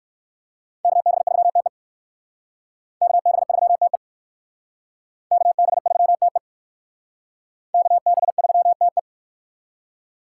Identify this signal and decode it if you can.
Morse code, heard as a keyed tone, about 64 wpm, 700 Hz. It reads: KB3TE KB3TE KB3TE KB3TE